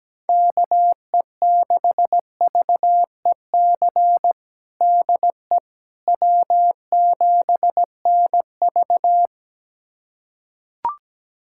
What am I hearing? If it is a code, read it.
Morse code, heard as a keyed tone, about 17 wpm, 700 Hz. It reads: KE6VEC DE W7NV
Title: USA Callsign DE Callsign 17wpm